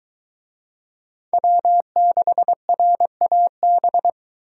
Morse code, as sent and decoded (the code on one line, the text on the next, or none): .-- -.... .-. .- -...
W6RAB